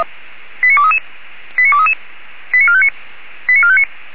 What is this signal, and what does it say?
Five-tone selective call (ZVEI), heard as quick runs of five tones, tones of about 70 ms. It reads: 89233 89233 89459 89459
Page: SDR/ZVEI